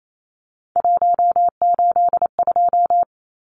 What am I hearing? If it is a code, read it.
Morse code, heard as a keyed tone, about 28 wpm, 700 Hz. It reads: 182
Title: Numbers - 3-Digit 28wpm